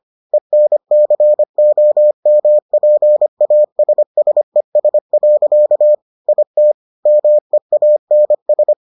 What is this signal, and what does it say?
Morse code, heard as a keyed tone, about 25 wpm, 600 Hz. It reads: ENCOMPASSES. IT MEANS